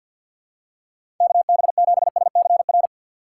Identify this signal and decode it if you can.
Morse code, heard as a keyed tone, about 46 wpm, 700 Hz. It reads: KB7SCR
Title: USA Call Signs - 6 Characters - 3x Character Spacing - 50wpm